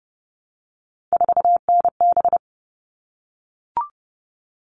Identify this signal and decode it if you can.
Morse code, heard as a keyed tone, about 30 wpm, 700 Hz. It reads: 4NB
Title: All Cut Numbers - 3-Digit 30wpm